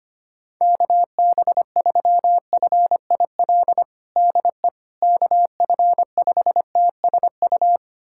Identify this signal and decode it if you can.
Morse code, heard as a keyed tone, about 25 wpm, 700 Hz. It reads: KB3FIL DE KF5TSU
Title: USA Callsign DE Callsign - 6 Characters - 25wpm